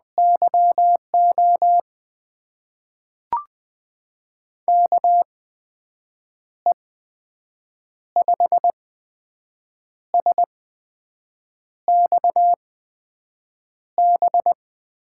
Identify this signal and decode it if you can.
Morse code, heard as a keyed tone, about 20 wpm, spaced as for 6 wpm, 700 Hz. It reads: YO KE5SXB
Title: USA Call Signs - 6 Characters - 8x Character Spacing - 20wpm